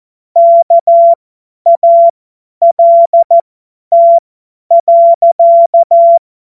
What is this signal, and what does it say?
Morse code, heard as a keyed tone, about 14 wpm, spaced as for 10 wpm, 680 Hz. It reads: KALT.
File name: Morsekurs / Witze / Witz 22. August 2023 50 F70